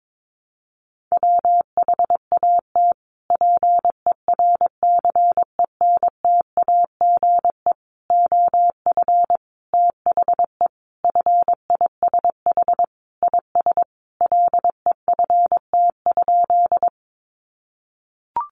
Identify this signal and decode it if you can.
Morse code, heard as a keyed tone, about 22 wpm, 700 Hz. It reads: WHAT PERCENTAGE OF THE FISH IS LEFT?
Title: Sentences from Top 1300 Words 22wpm